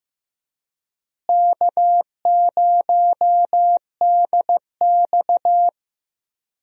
Morse code, heard as a keyed tone, about 15 wpm, 700 Hz. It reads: K0DX
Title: USA Call Signs - 4 Characters - Repeated 2x - 15wpm